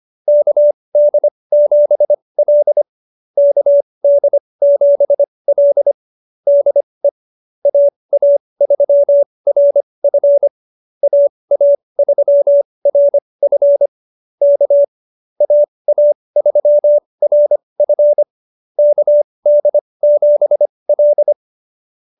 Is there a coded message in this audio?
Yes: Morse code from 0.3 to 21.3 s, heard as a keyed tone, about 25 wpm, spaced as for 20 wpm, 590 Hz. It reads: KD7L KD7L DE AA3RF AA3RF K AA3RF KD7L